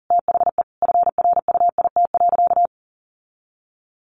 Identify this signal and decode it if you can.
Morse code, heard as a keyed tone, about 40 wpm, 700 Hz. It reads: THE FRUIT.
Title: Sentences from Top 1300 Words 40wpm